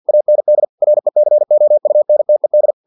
Morse code, heard as a keyed tone, about 50 wpm, 600 Hz. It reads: AND RECOUNTED
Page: Never 49wpm